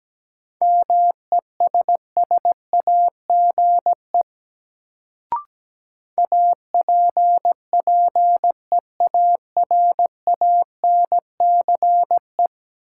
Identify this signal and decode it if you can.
Morse code, heard as a keyed tone, about 17 wpm, 700 Hz. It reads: MESSAGE APPEARANCE
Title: Sentences from Top 1100 Words - New Word Drill Alt - 17wpm